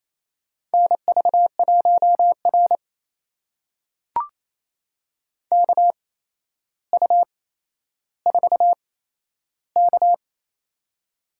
Morse code, heard as a keyed tone, about 28 wpm, spaced as for 12 wpm, 700 Hz. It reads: NV1R KU4K
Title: USA Call Signs - 4 Characters - 8x Character Spacing - 28wpm